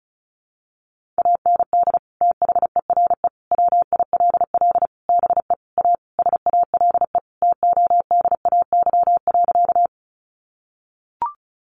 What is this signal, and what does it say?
Morse code, heard as a keyed tone, about 34 wpm, 700 Hz. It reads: AND THERE WILL BE A SALE TODAY.
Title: Sentences from Top 1400 Words 35wpm